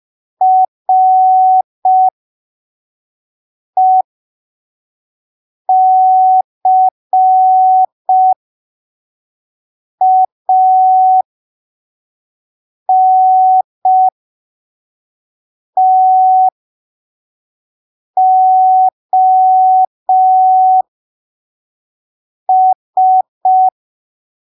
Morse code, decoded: RECANTOS